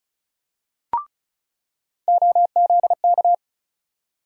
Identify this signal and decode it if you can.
Morse code, heard as a keyed tone, about 36 wpm, 700 Hz. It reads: OZK
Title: USA Call Sign Postfixes - 3 Letters 35wpm